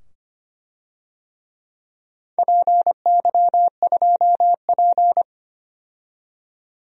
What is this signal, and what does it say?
Morse code, heard as a keyed tone, about 25 wpm, 700 Hz. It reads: PY2P
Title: DXCC-100 Call Signs - 4 Characters - 25wpm